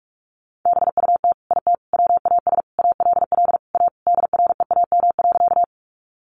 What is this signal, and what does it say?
Morse code, heard as a keyed tone, about 44 wpm, 700 Hz. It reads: BUT IT WAS ALL A DREAM.
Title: Sentences from Top 1500 Words 45wpm